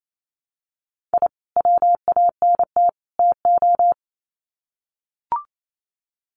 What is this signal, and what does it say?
Morse code, heard as a keyed tone, about 28 wpm, 700 Hz. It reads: I WANT TO
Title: Sets of 3 Words 28wpm